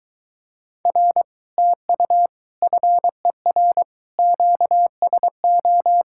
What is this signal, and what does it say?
Morse code, heard as a keyed tone, about 22 wpm, 700 Hz. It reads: R TU FER QSO